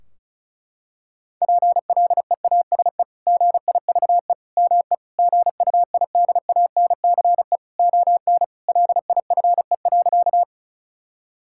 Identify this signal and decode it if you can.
Morse code, heard as a keyed tone, about 36 wpm, 700 Hz. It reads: PLEASE GIVE ME GUIDANCE ON LIFE.
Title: Sentences from Top 1600 Words 35wpm